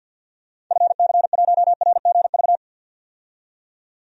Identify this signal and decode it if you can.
Morse code, heard as a keyed tone, about 50 wpm, 700 Hz. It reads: FC1RGV
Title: DXCC-100 Call Signs - 6 Characters - 50wpm